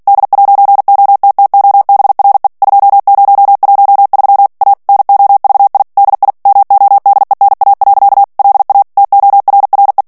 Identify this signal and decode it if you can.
Morse code, heard as a keyed tone, about 48 wpm, 800 Hz. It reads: D'OTTOBRE 2013 A NOVI DI MODENA. LA TORRE
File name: morse mp3 / ITALIEN texte clair 1 vitesse 45 mots minute